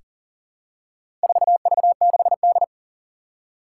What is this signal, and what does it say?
Morse code, heard as a keyed tone, about 40 wpm, 700 Hz. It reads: 4V6D
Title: Four Challenging Letter-Number Combinations 40wpm